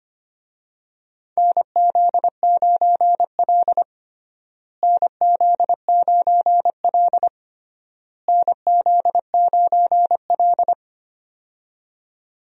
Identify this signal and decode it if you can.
Morse code, heard as a keyed tone, about 25 wpm, 700 Hz. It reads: NZ9L NZ9L NZ9L
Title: USA Call Signs - 4 Characters - Repeated 3x - 25wpm